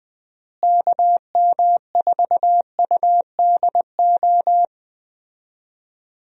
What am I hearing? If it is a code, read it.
Morse code, heard as a keyed tone, about 20 wpm, 700 Hz. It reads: KM4UDO
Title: USA Call Signs - 6 Characters - Repeated 2x - 20wpm